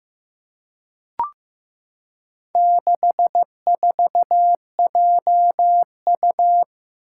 Morse code, heard as a keyed tone, about 15 wpm, 700 Hz. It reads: 64JU
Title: Four Challenging Letter-Number Combinations 15wpm